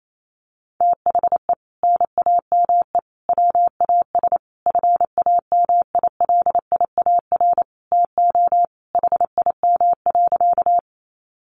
Morse code, heard as a keyed tone, about 28 wpm, 700 Hz. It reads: THE NAME WAS FAMILIAR TO HIM.
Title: Sentences from Top 1000 Words 28wpm